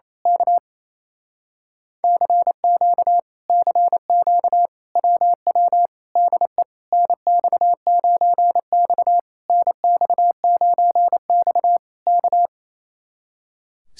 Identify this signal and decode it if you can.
Morse code, heard as a keyed tone, about 27 wpm, 700 Hz. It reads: K CQ CQ WW DE NX9X NX9X K
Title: Speed-Racing Common QSO Phrases - 28wpm